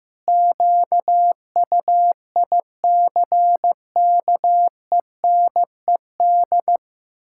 Morse code, heard as a keyed tone, about 15 wpm, 700 Hz. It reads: QUICKENED
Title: Pangram Challenge v2 - 15wpm